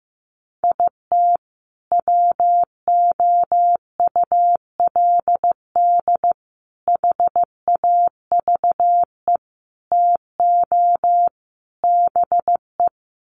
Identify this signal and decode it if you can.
Morse code, heard as a keyed tone, about 15 wpm, 700 Hz. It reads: IT WOULD HAVE TO BE
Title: Sets of 5 Words Encore 15wpm